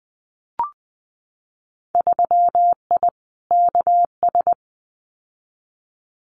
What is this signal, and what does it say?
Morse code, heard as a keyed tone, about 20 wpm, 700 Hz. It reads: 3I KS
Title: Winter Field Day Exchanges 20wpm